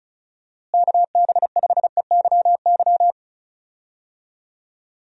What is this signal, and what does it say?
Morse code, heard as a keyed tone, about 36 wpm, 700 Hz. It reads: KB5EYY
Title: USA Call Signs - 6 Characters - Repeated 2x - 35wpm